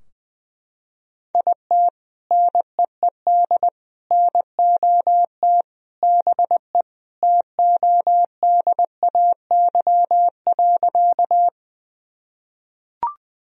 Morse code, heard as a keyed tone, about 20 wpm, 700 Hz. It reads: IT NEED NOT BE TODAY.
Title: Sentences from Top 1000 Words - Review Part 4 - 20wpm